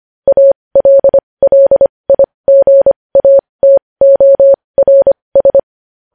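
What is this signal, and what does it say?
Morse code, heard as a keyed tone, about 25 wpm, spaced as for 20 wpm, 550 Hz. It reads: ALLIGATORS
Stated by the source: Morse code practice of news headlines at 20 wpm.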